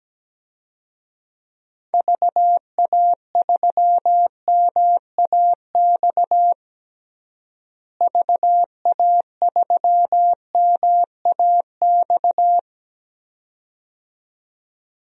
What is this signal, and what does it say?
Morse code, heard as a keyed tone, about 17 wpm, 700 Hz. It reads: VA3MAX VA3MAX
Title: DXCC-100 Call Signs - 6 Characters - Repeated 2x - 17wpm